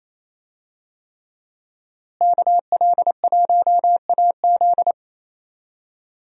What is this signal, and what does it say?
Morse code, heard as a keyed tone, about 28 wpm, 700 Hz. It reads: KL1AZ
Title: USA Call Signs - 5 Characters - Repeated 3x - 28wpm